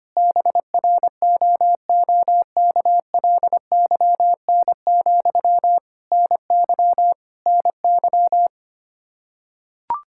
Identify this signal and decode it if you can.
Morse code, heard as a keyed tone, about 25 wpm, 700 Hz. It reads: BROOKLYN, NY NY
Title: K1USN SST QSO Practice 25wpm